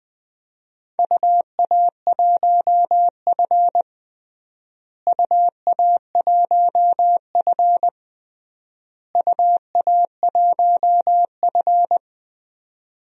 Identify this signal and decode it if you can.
Morse code, heard as a keyed tone, about 21 wpm, 700 Hz. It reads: UA1F UA1F UA1F
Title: DXCC-100 Call Signs - 4 Characters - Repeated 3x - 20wpm